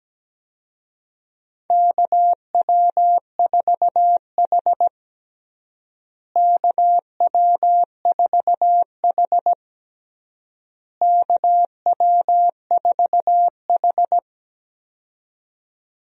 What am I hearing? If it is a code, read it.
Morse code, heard as a keyed tone, about 17 wpm, 700 Hz. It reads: KW4H KW4H KW4H
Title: USA Call Signs - 4 Characters - Repeated 3x - 17wpm